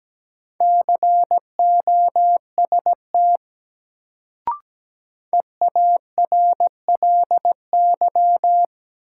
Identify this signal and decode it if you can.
Morse code, heard as a keyed tone, about 17 wpm, 700 Hz. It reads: COST EARLY
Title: Sets of 3 Words - New Word Drill - 17wpm